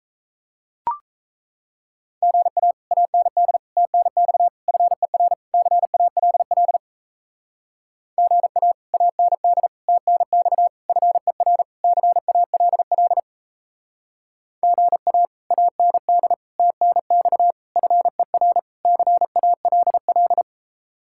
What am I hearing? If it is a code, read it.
Morse code, heard as a keyed tone, about 34 wpm, 700 Hz. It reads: GA AND TNX FER CALL GA AND TNX FER CALL GA AND TNX FER CALL
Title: Speed-Racing Common QSO Phrases - 28wpm